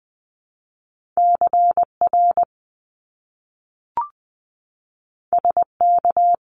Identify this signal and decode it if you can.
Morse code, heard as a keyed tone, about 20 wpm, 700 Hz. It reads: CR SK
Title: Top Two Letter Combinations in Words 20wpm